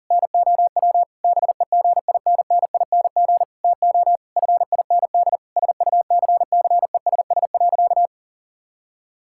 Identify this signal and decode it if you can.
Morse code, heard as a keyed tone, about 40 wpm, 700 Hz. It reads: NOW BEGINNING TO FIND SUCCESS.
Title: Sentences from Top 1000 Words - Review Part 5 - 40wpm